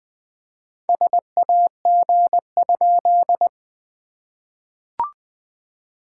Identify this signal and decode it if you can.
Morse code, heard as a keyed tone, about 19 wpm, 700 Hz. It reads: SAG?
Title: SOTA QSO Practice 20wpm